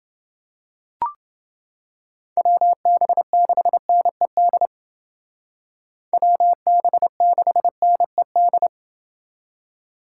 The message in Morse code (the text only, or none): WB6NED WB6NED